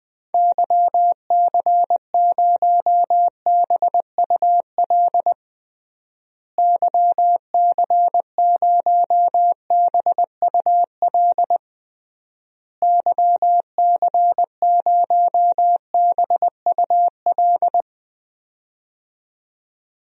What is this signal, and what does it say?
Morse code, heard as a keyed tone, about 20 wpm, 700 Hz. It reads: YC0BUL YC0BUL YC0BUL
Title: DXCC-100 Call Signs - 6 Characters - Repeated 3x - 20wpm